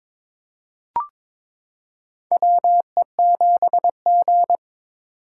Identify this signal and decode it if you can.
Morse code, heard as a keyed tone, about 22 wpm, 700 Hz. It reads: WE7G
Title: K1USN SST QSO Practice - 22wpm